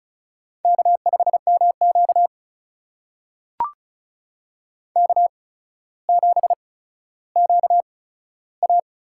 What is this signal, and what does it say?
Morse code, heard as a keyed tone, about 36 wpm, spaced as for 15 wpm, 700 Hz. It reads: K5MQ K7QA
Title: USA Call Signs - 4 Characters - 8x Character Spacing - 35wpm